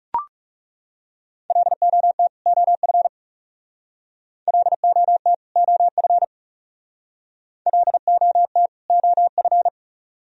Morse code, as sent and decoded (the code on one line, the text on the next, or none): .-.. --- - / --- ..-. / .-.. --- - / --- ..-. / .-.. --- - / --- ..-.
LOT OF LOT OF LOT OF